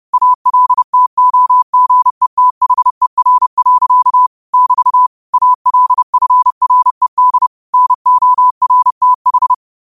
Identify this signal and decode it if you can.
Morse code, heard as a keyed tone, about 30 wpm, 1000 Hz. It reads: ALTOGETHER. <BT> ALFRED NORTH
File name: qotdfiles / 30wpm